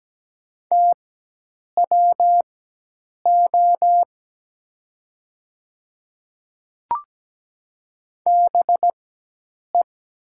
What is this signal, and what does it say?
Morse code, heard as a keyed tone, about 17 wpm, spaced as for 8 wpm, 690 Hz. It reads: TWO BE
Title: 4x Character Spacing - Top 100 Words - 17wpm